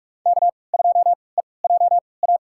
Morse code, heard as a keyed tone, about 42 wpm, spaced as for 25 wpm, 700 Hz. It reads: K2EJA
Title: USA Call Signs - 5 Characters - 3x Character Spacing - 45wpm